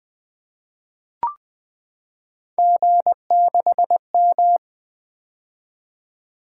Morse code, heard as a keyed tone, about 20 wpm, 700 Hz. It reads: G6M
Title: Three Letter-Number Combinations - 20wpm